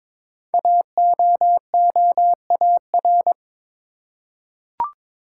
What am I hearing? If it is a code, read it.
Morse code, heard as a keyed tone, about 22 wpm, 700 Hz. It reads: AOOAR
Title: Code Groups - 5 Letter:Numbers - 22wpm